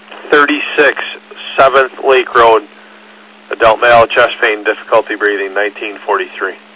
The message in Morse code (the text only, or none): none